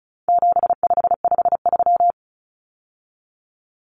Morse code, heard as a keyed tone, about 34 wpm, 700 Hz. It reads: 7553
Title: Numbers - 4-Digit 35wpm